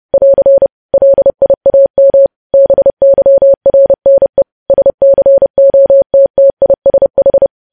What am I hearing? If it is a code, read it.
Morse code, heard as a keyed tone, about 30 wpm, 550 Hz. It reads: <AR> LIAM BYRNE SCOTTISH
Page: news headlines in morse code at 30 wpm